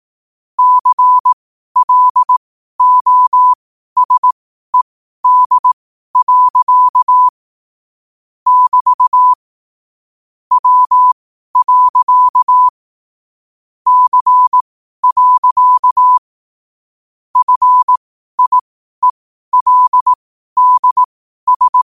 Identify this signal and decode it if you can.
Morse code, heard as a keyed tone, about 18 wpm, spaced as for 12 wpm, 1000 Hz. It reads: CLOSED. <BT> W. C. FIELDS
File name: qotdfiles / 12wpm